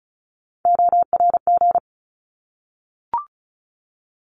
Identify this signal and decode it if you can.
Morse code, heard as a keyed tone, about 34 wpm, 700 Hz. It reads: ORG
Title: ARRL Sections 35wpm